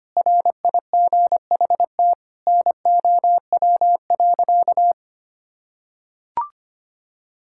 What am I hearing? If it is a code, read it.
Morse code, heard as a keyed tone, about 25 wpm, 700 Hz. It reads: RIGHT NOW.
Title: Sentences from Top 1000 Words - Review Part 6 - 25wpm